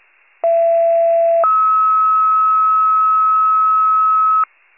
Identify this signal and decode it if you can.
Two-tone page, heard as a one-second tone, then a three-second tone, about 670 then 1290 Hz. It is heard from 0.4 to 4.4 s.
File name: wcalert